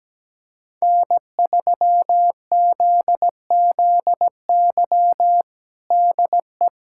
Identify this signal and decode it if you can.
Morse code, heard as a keyed tone, about 17 wpm, 700 Hz. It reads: N3ZZY DE
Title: USA Callsign DE Callsign - 5 Characters - 17wpm